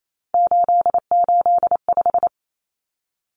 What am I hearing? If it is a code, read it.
Morse code, heard as a keyed tone, about 28 wpm, 700 Hz. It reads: 885
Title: Numbers - 3-Digit 28wpm